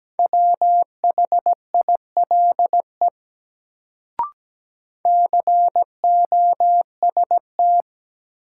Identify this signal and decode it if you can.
Morse code, heard as a keyed tone, about 17 wpm, 700 Hz. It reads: WHILE COST
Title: Sets of 3 Words - New Word Drill - 17wpm